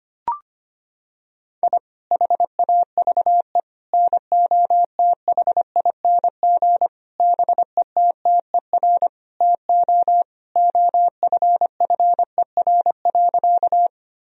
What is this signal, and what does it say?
Morse code, heard as a keyed tone, about 25 wpm, 690 Hz. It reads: I HAVE NOTHING BETTER TO OFFER.
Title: Sentences from Top 1000 Words - Review Part 4 - 25wpm